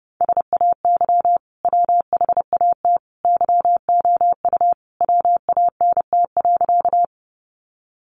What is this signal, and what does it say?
Morse code, heard as a keyed tone, about 28 wpm, 700 Hz. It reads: SAY WHAT YOU WANT.
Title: Sentences from Top 200 Words 30wpm